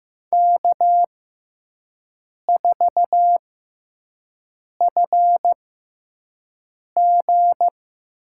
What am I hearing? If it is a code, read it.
Morse code, heard as a keyed tone, about 15 wpm, spaced as for 5 wpm, 700 Hz. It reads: K4FG
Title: USA Call Signs - 4 Characters - 6x Character Spacing - 15wpm